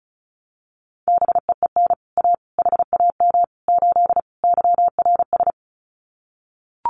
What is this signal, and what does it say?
Morse code, heard as a keyed tone, about 34 wpm, 700 Hz. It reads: BEEN A HAM 8 YRS
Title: Common QSO Phrases Encore 35wpm